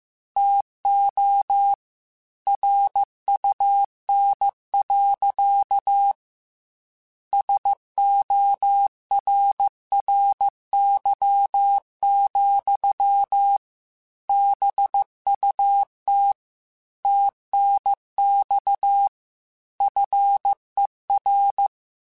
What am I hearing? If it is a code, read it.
Morse code, heard as a keyed tone, about 15 wpm, 800 Hz. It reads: TO RUN. SORRY, BUT TNX FER